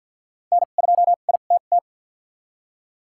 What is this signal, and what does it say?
Morse code, heard as a keyed tone, about 50 wpm, spaced as for 36 wpm, 700 Hz. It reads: N2ITT
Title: USA Call Signs - 5 Characters - 2x Character Spacing - 50wpm